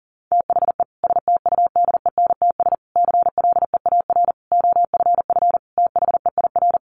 Morse code, heard as a keyed tone, about 37 wpm, 700 Hz. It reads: THE STUDENTS CLEAR OFF THEIR
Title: Additional Easy Phrases 40wpm